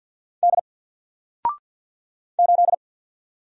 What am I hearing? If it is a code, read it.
Morse code, heard as a keyed tone, about 48 wpm, 700 Hz. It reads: D 8
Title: Single Letter-Number Alt 50wpm